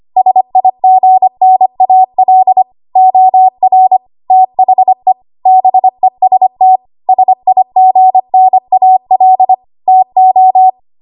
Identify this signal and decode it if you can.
Morse code, heard as a keyed tone, about 25 wpm, 750 Hz. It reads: SIGNAL OR THE BEST SIGNAL TO
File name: Archive / 25 WPM / 25WPM